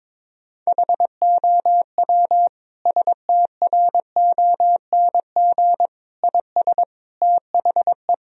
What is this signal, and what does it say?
Morse code, heard as a keyed tone, about 22 wpm, 700 Hz. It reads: HOW STRONG IS THE
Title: Sentences from Top 1000 Words - Review Part 4 - 22wpm